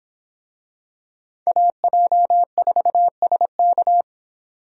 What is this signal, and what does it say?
Morse code, heard as a keyed tone, about 26 wpm, 700 Hz. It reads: AJ4SK